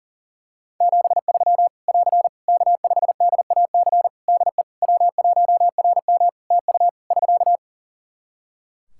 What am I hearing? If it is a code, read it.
Morse code, heard as a keyed tone, about 40 wpm, 700 Hz. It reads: 73 <AR> K5DAC DE W1RM TU <SK>
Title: Practice QSO - Exchange 4 - Ending - 40wpm